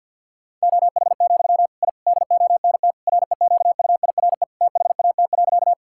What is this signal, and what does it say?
Morse code, heard as a keyed tone, about 50 wpm, 700 Hz. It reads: OH, I DONT REQUIRE THAT.
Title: Sentences from Top 1000 Words - Review Part 6 - 50wpm